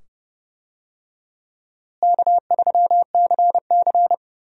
Morse code, heard as a keyed tone, about 30 wpm, 700 Hz. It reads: K3CC
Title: USA Call Signs - 4 Characters - Repeated 2x - 30wpm